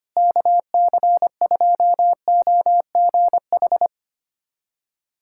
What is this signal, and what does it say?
Morse code, heard as a keyed tone, about 25 wpm, 700 Hz. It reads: KC2OGH
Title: USA Callsign DE Callsign 25wpm